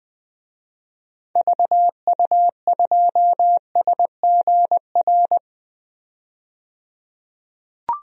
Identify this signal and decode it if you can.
Morse code, heard as a keyed tone, about 20 wpm, 700 Hz. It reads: VU2SGR
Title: DXCC-100 Call Signs - 6 Characters - 20wpm